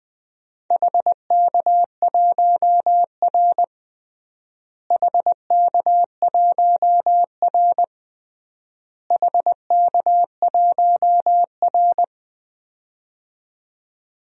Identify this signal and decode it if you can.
Morse code, heard as a keyed tone, about 20 wpm, 700 Hz. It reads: HK1R HK1R HK1R
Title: DXCC-100 Call Signs - 4 Characters - Repeated 3x - 20wpm